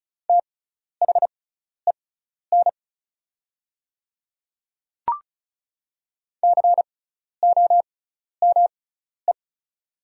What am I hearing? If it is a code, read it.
Morse code, heard as a keyed tone, about 36 wpm, spaced as for 12 wpm, 700 Hz. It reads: THEN COME
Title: 6x Character Spacing - Top 100 Words - 35wpm